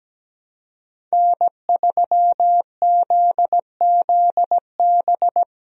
Morse code, heard as a keyed tone, about 17 wpm, 700 Hz. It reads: N3ZZB
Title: USA Call Signs - 5 Characters - Repeated 3x - 17wpm